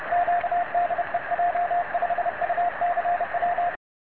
Morse code, heard as a keyed tone, about 30 wpm, 700 Hz, with radio noise. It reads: Q DE J5UCW
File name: j5ucw-cq-cw